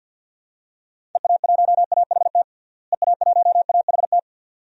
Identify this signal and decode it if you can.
Morse code, heard as a keyed tone, about 46 wpm, 700 Hz. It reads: EA1AHT EA1AHT
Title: DXCC-100 Call Signs - 6 Characters - Repeated 2x - 50wpm